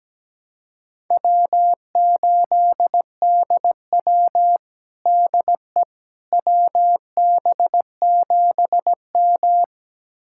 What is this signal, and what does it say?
Morse code, heard as a keyed tone, about 17 wpm, 700 Hz. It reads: W8DW DE WB7M
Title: USA Callsign DE Callsign - 4 Characters - 17wpm